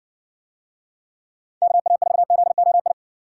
Morse code, heard as a keyed tone, about 60 wpm, 700 Hz. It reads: KA4ZOI